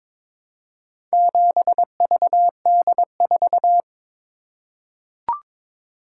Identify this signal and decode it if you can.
Morse code, heard as a keyed tone, about 22 wpm, 700 Hz. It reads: 7VD4
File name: practice / 22wpm / Four Letter-Number Combinations - 22wpm